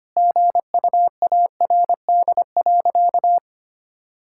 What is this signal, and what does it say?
Morse code, heard as a keyed tone, about 25 wpm, 700 Hz. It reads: GUARD.
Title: Sentences from Top 1000 Words - Review Part 5 - 25wpm